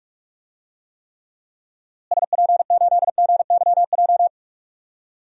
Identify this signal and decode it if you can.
Morse code, heard as a keyed tone, about 45 wpm, 700 Hz. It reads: SP8GYJ
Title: DXCC-100 Call Signs - 6 Characters - Repeated 2x - 45wpm